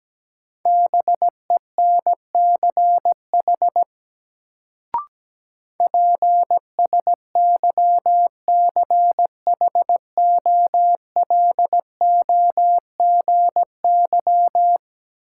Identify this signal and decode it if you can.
Morse code, heard as a keyed tone, about 17 wpm, 700 Hz. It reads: BENCH PSYCHOLOGY
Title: Sentences from Top 1400 Words - New Word Drill Alt - 17wpm